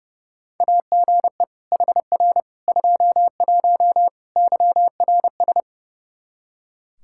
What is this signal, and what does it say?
Morse code, heard as a keyed tone, about 30 wpm, 700 Hz. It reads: AGE HR 21 YRS
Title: K1USN SST QSO Practice 30wpm